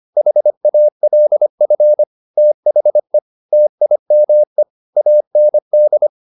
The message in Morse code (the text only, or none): HALF THE TIME AND